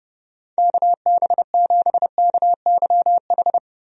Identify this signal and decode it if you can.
Morse code, heard as a keyed tone, about 30 wpm, 700 Hz. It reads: KB7KYH
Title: USA Call Signs - 6 Characters - 30wpm